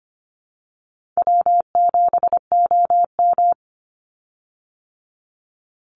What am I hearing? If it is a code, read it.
Morse code, heard as a keyed tone, about 24 wpm, 700 Hz. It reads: W7OM
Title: Easy Call Signs - 25wpm